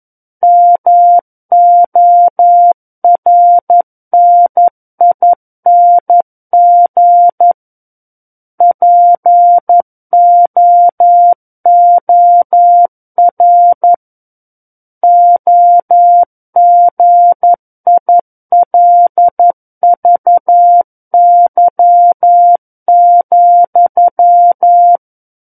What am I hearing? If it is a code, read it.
Morse code, heard as a keyed tone, about 11 wpm, 700 Hz. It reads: MORNING POOR OGILVY,
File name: 11 WPM